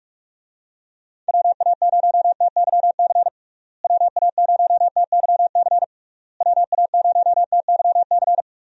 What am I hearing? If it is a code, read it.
Morse code, heard as a keyed tone, about 41 wpm, 700 Hz. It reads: WA0TYC WA0TYC WA0TYC
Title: USA Call Signs - 6 Characters - Repeated 3x - 45wpm